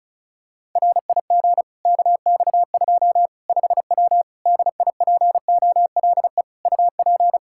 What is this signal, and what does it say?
Morse code, heard as a keyed tone, about 34 wpm, 700 Hz. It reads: RIG KX2 5W DIPOLE UP
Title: Practice QSO - Exchange 2 - Info - 35wpm